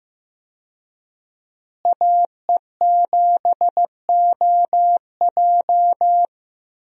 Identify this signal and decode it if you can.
Morse code, heard as a keyed tone, about 15 wpm, 700 Hz. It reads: AE7OJ
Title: USA Call Signs - 5 Characters - Repeated 3x - 15wpm